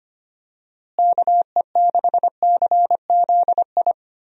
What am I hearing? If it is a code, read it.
Morse code, heard as a keyed tone, about 25 wpm, 700 Hz. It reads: KE6CZI
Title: USA Call Signs - 6 Characters - Repeated 2x - 25wpm